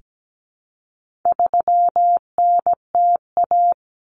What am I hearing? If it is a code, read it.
Morse code, heard as a keyed tone, about 17 wpm, 700 Hz. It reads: 3NTA
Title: Common Cut Numbers - 4-Digit 17wpm